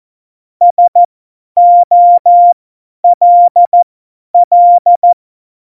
Morse code, heard as a keyed tone, about 14 wpm, spaced as for 10 wpm, 700 Hz. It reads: SOLL
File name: Morsekurs / Witze / Witz 05. Januar 2021, 50 F70